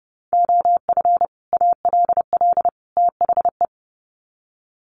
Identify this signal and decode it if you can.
Morse code, heard as a keyed tone, about 30 wpm, 700 Hz. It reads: OF ALL THE
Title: Sets of 3 Words 30wpm